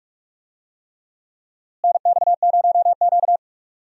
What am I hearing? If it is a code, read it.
Morse code, heard as a keyed tone, about 46 wpm, 700 Hz. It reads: NX0Q
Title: USA Call Signs - 4 Characters - Repeated 3x - 45wpm